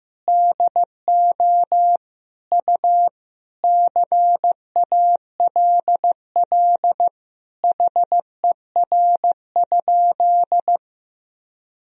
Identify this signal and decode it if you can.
Morse code, heard as a keyed tone, about 15 wpm, 700 Hz. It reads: DO U CALL HER?
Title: Sentences from Top 100 Words and Abbreviations 15wpm